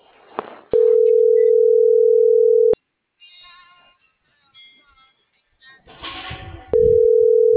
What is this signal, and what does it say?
Ringback tone (North America), 440 + 480 Hz.